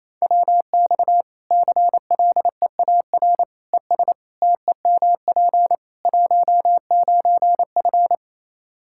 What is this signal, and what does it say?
Morse code, heard as a keyed tone, about 27 wpm, 700 Hz. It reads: WX CLEAR ES TEMP 19F
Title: Practice QSO - Exchange 2 - Info - 28wpm